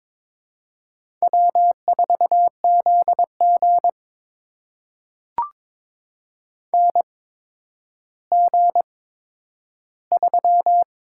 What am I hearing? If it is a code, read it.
Morse code, heard as a keyed tone, about 22 wpm, spaced as for 11 wpm, 700 Hz. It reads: W4ZG NG3
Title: USA Call Signs - 4 Characters - 8x Character Spacing - 22wpm